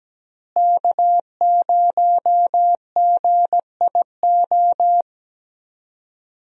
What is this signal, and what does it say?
Morse code, heard as a keyed tone, about 17 wpm, 700 Hz. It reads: K0GIO
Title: USA Call Signs - 5 Characters - Repeated 3x - 17wpm